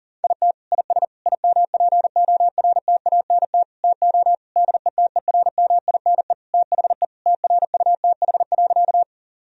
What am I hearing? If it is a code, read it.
Morse code, heard as a keyed tone, about 37 wpm, 700 Hz. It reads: IT IS IMPORTANT TO DETERMINE THE TRUTH.
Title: Sentences from Top 1000 Words - Review Part 4 - 40wpm